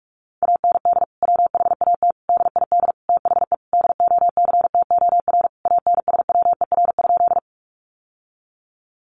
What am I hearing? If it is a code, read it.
Morse code, heard as a keyed tone, about 44 wpm, 700 Hz. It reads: AND WHAT DID THE DOCTOR ANSWER?
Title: Sentences from Top 1500 Words 45wpm